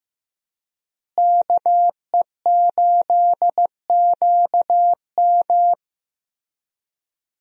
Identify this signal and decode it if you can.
Morse code, heard as a keyed tone, about 15 wpm, 700 Hz. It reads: KE8QM
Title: USA Call Signs - 5 Characters - Repeated 3x - 15wpm